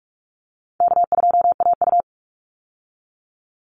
Morse code, heard as a keyed tone, about 44 wpm, 700 Hz. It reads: K2AU
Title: Easy Call Signs Repeated 3x - 45wpm